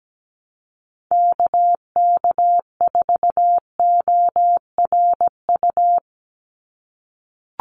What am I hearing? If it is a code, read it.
Morse code, heard as a keyed tone, about 17 wpm, 700 Hz. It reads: KK4ORU
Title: USA Call Signs 17wpm